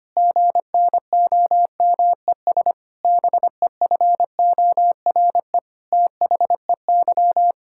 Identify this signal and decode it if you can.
Morse code, heard as a keyed tone, about 25 wpm, 700 Hz. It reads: GNOMES BEFORE THEY
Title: Pangram Challenge v2 - 25wpm